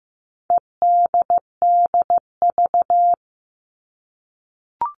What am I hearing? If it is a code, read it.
Morse code, heard as a keyed tone, about 15 wpm, 700 Hz. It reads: EDDV
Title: All Cut Numbers - 4-Digit 15wpm